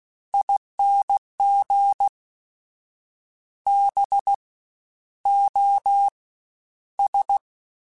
Morse code, heard as a keyed tone, about 16 wpm, 800 Hz. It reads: ING B O S
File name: Special INGs-mp3s / Special INGs-mp3s / SlowToFaster ing @15